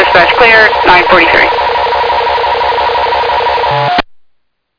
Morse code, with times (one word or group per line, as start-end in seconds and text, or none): none